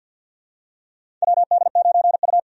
Morse code, heard as a keyed tone, about 50 wpm, 700 Hz. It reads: WD9U
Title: USA Call Signs - 4 Characters - 50wpm